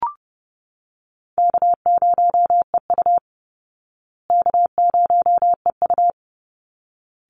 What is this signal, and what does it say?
Morse code, heard as a keyed tone, about 30 wpm, 700 Hz. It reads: K0EU K0EU
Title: Call Signs Repeated 3x - 30wpm